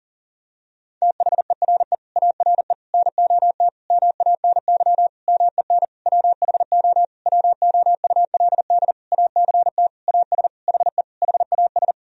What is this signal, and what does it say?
Morse code, heard as a keyed tone, about 40 wpm, 700 Hz. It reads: THERE ARE NOT MANY MEN WHO WOULD ACT AS HE HAS
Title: Sentences from Top 1000 Words - Review Part 1 - 40wpm